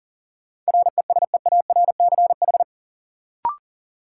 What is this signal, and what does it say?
Morse code, heard as a keyed tone, about 40 wpm, 700 Hz. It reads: RESEARCH
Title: Sentences from Top 1400 Words - New Word Drill Alt - 40wpm